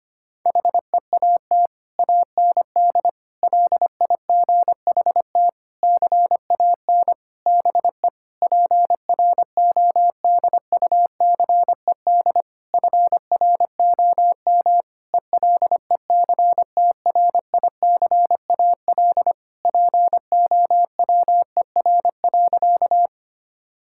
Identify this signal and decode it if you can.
Morse code, heard as a keyed tone, about 25 wpm, 700 Hz. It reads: HEAT AND LIGHT CAN BE PRODUCED FROM ELECTRICAL POWER.
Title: Sentences from Top 1000 Words - Review Part 6 - 25wpm